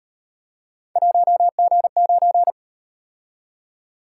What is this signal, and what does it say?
Morse code, heard as a keyed tone, about 37 wpm, 700 Hz. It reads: 1G9